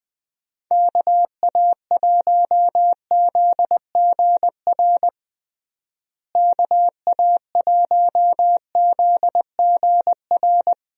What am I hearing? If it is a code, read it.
Morse code, heard as a keyed tone, about 21 wpm, 700 Hz. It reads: KA1ZGR KA1ZGR
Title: USA Call Signs - 6 Characters - Repeated 3x - 20wpm